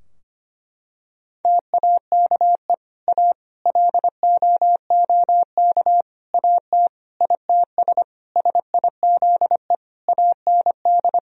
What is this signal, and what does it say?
Morse code, heard as a keyed tone, about 25 wpm, 700 Hz. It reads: TAKE A LOOK AT ITS SIZE AND
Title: Sentences from Top 1000 Words - Review Part 2 - 25wpm